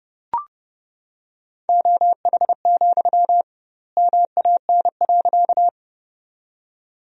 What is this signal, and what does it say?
Morse code, heard as a keyed tone, about 32 wpm, 700 Hz. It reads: OH, MAN.
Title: 2x Word-Spacing - Sentences from Top 500 Words - 30wpm